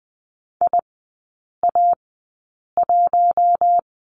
Morse code, heard as a keyed tone, about 20 wpm, spaced as for 8 wpm, 700 Hz. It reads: IA1
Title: 047 Lesson 9e - Number 4 - Three Characters 20wpm